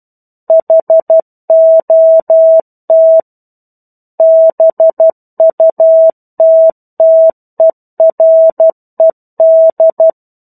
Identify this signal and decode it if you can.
Morse code, heard as a keyed tone, about 12 wpm, 650 Hz. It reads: HOT BUTTERED
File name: 01-Chapter-01 12 WPM